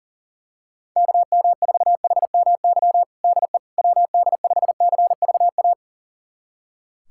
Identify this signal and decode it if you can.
Morse code, heard as a keyed tone, about 37 wpm, 700 Hz. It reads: KM4HMY DE WD5CVA
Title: USA Callsign DE Callsign - 6 Characters - 40wpm